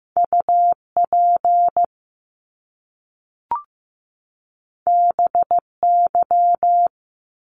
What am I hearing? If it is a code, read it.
Morse code, heard as a keyed tone, about 15 wpm, 700 Hz. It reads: UP BY
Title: Top Two Letter Combinations in Words 15wpm